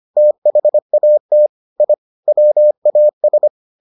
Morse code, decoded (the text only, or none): THAT I WAS